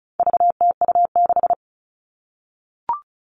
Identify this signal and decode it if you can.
Morse code, heard as a keyed tone, about 34 wpm, 700 Hz. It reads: VTU6
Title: All Cut Numbers - 4-Digit 35wpm